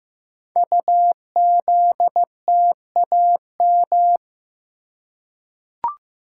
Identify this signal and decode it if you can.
Morse code, heard as a keyed tone, about 15 wpm, 700 Hz. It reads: UZTAM
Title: Code Groups - 5 Letters - 15wpm